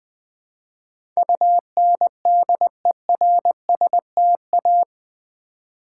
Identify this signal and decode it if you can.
Morse code, heard as a keyed tone, about 20 wpm, 700 Hz. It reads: UNDERSTA
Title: Build A Word - 4 Letters At A Time - 10-Letter Words - 20wpm